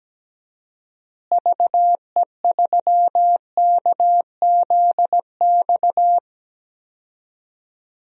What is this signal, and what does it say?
Morse code, heard as a keyed tone, about 17 wpm, 700 Hz. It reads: VE3KZX
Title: DXCC-100 Call Signs - 6 Characters - Repeated 3x - 17wpm